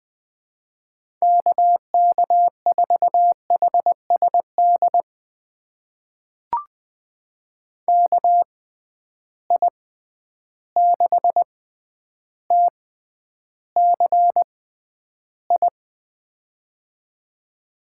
Morse code, heard as a keyed tone, about 20 wpm, spaced as for 10 wpm, 700 Hz. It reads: KK4HSD KI6TCI
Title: USA Call Signs - 6 Characters - 6x Character Spacing - 20wpm